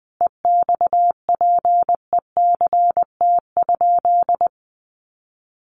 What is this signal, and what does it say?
Morse code, heard as a keyed tone, about 19 wpm, 700 Hz. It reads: EXPECT?
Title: Sentences from Top 1300 Words 20wpm